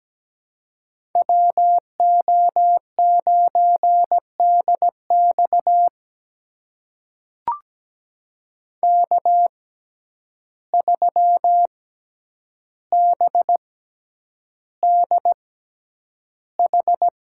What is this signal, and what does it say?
Morse code, heard as a keyed tone, about 17 wpm, spaced as for 9 wpm, 700 Hz. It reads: WO9DX K3BDH
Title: USA Call Signs - 5 Characters - 6x Character Spacing - 17wpm